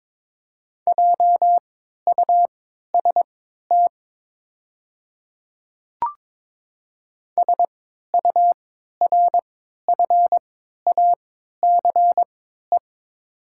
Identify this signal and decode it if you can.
Morse code, heard as a keyed tone, about 22 wpm, spaced as for 12 wpm, 700 Hz. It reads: JUST SURFACE
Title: 3x Character Spacing - Top 500 Words - 22wpm